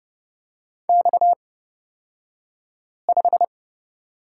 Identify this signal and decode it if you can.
Morse code, heard as a keyed tone, about 30 wpm, spaced as for 5 wpm, 700 Hz. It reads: X5
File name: Lesson 37d - Letter x - Two Characters 30wpm